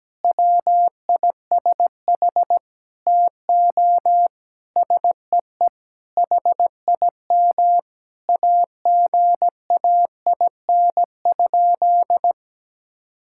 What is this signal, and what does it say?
Morse code, heard as a keyed tone, about 17 wpm, 700 Hz. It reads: WISH TO SEE HIM AGAIN?
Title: Sentences from Top 1600 Words 17wpm